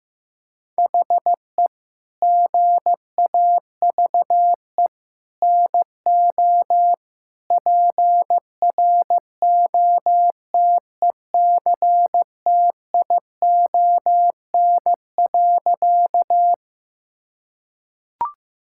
Morse code, heard as a keyed tone, about 15 wpm, 700 Hz. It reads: HE GAVE NO PROTECTION.
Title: Sentences from Top 1600 Words 15wpm